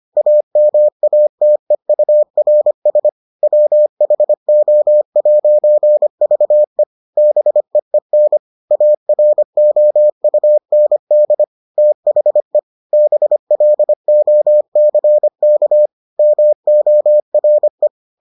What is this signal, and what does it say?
Morse code, heard as a keyed tone, about 25 wpm, 600 Hz. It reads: AMATEURS WHO'VE BEEN AROUND THE BLOCK MORE